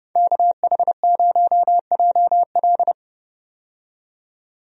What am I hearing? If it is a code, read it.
Morse code, heard as a keyed tone, about 32 wpm, 700 Hz. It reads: KH0JL
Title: USA Call Signs - 5 Characters - 30wpm